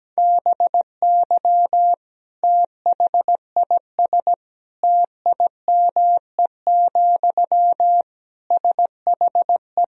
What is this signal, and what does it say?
Morse code, heard as a keyed tone, about 17 wpm, 700 Hz. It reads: BY THIS TIME, SHE
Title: Sentences from Top 1000 Words - Review Part 4 - 17wpm